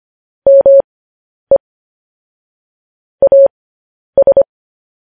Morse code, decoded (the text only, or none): ME AS